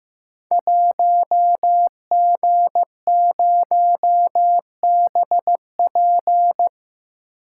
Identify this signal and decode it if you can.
Morse code, heard as a keyed tone, about 15 wpm, 700 Hz. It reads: 1G0BP
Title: Code Groups - 5 Letter:Numbers - 15wpm